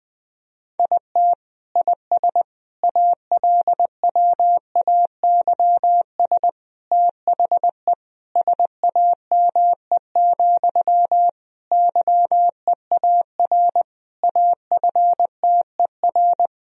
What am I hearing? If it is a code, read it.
Morse code, heard as a keyed tone, about 20 wpm, 700 Hz. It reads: IT IS ALWAYS THE SAME, YEAR AFTER
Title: Sentences from Top 1000 Words - Review Part 1 - 20wpm